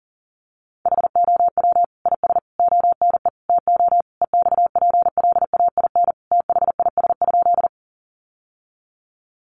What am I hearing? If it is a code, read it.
Morse code, heard as a keyed tone, about 40 wpm, 700 Hz. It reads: HOW IS ONE TO EXPLAIN THIS?
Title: Sentences from Top 1100 Words 40wpm